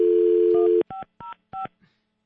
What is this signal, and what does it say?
DTMF, heard as telephone touch tones, tones of about 140 ms. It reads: 1505